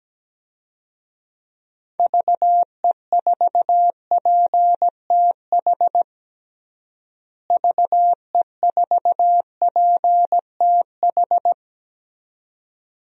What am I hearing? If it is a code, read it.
Morse code, heard as a keyed tone, about 17 wpm, 700 Hz. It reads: VE4PTH VE4PTH
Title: DXCC-100 Call Signs - 6 Characters - Repeated 2x - 17wpm